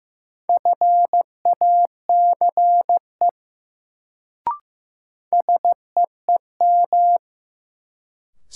Morse code, heard as a keyed tone, about 15 wpm, 700 Hz. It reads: FACE SEEM
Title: Sentences from Top 300 Words - New Word Drill - 15wpm